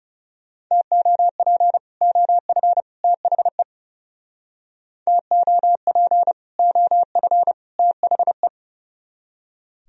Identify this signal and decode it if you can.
Morse code, heard as a keyed tone, about 38 wpm, 700 Hz. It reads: TOP OF THE TOP OF THE
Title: Speed-Racing - Sets of 3 Words 30wpm